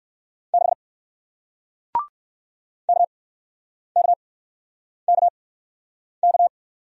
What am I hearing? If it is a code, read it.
Morse code, heard as a keyed tone, about 61 wpm, spaced as for 9 wpm, 700 Hz. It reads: 6 KKKK